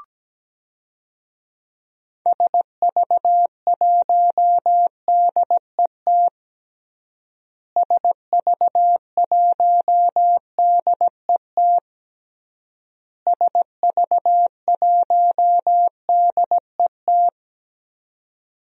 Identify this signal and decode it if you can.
Morse code, heard as a keyed tone, about 17 wpm, 700 Hz. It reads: SV1DET SV1DET SV1DET
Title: DXCC-100 Call Signs - 6 Characters - Repeated 3x - 17wpm